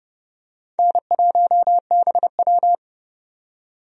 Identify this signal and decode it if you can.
Morse code, heard as a keyed tone, about 31 wpm, 700 Hz. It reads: N1BW
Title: USA Call Signs - 4 Characters - 30wpm